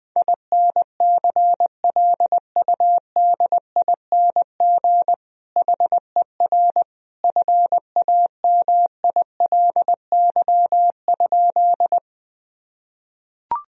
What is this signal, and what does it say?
Morse code, heard as a keyed tone, about 21 wpm, 700 Hz. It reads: INCLUDING HER FAMILY?
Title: Sentences from Top 1000 Words - Review Part 4 - 20wpm